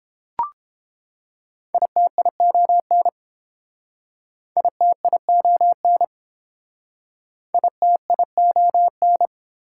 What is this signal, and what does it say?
Morse code, heard as a keyed tone, about 30 wpm, 700 Hz. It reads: ITION ITION ITION